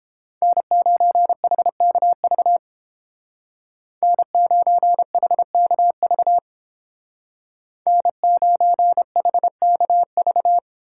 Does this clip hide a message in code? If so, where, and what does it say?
Yes: Morse code from 0.4 to 10.6 s, heard as a keyed tone, about 28 wpm, 700 Hz. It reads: N9HKV N9HKV N9HKV